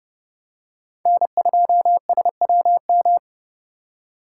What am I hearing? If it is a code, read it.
Morse code, heard as a keyed tone, about 30 wpm, 700 Hz. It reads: N2SWM
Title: USA Call Signs - 5 Characters - 2x Character Spacing - 30wpm